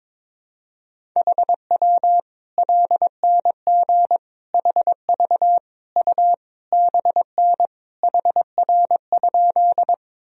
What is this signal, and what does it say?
Morse code, heard as a keyed tone, about 22 wpm, 700 Hz. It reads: HW LNG HV U BN HR?
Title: Sentences from Top 200 Words and Abbreviations 22wpm